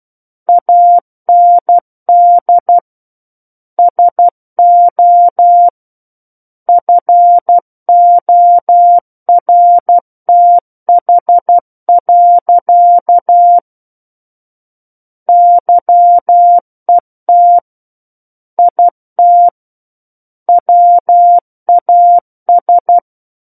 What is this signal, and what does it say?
Morse code, heard as a keyed tone, about 12 wpm, 700 Hz. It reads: AND SO FORTH. YET IT WAS
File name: Book One / 03-Chapter 3 - 12 WPM